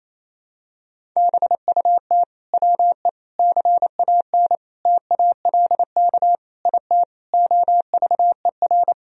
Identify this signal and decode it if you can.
Morse code, heard as a keyed tone, about 28 wpm, 700 Hz. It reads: BUT WE CAN TALK IT OVER
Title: Sentences from Top 1000 Words - Review Part 4 - 28wpm